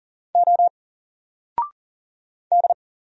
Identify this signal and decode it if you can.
Morse code, heard as a keyed tone, about 40 wpm, 700 Hz. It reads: O D
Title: Single Letter-Number Alt 40wpm